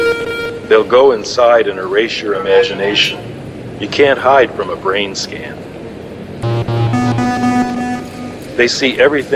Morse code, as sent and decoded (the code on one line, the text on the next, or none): none